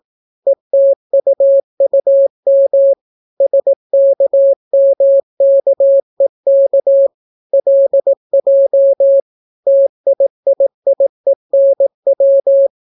Morse code, heard as a keyed tone, about 18 wpm, 550 Hz. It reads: ETUUM SKMKEK LJ TIIIENW